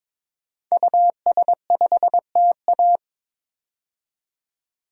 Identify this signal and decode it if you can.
Morse code, heard as a keyed tone, about 21 wpm, 700 Hz. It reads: US5TA
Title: DXCC-100 Call Signs - 5 Characters - 22wpm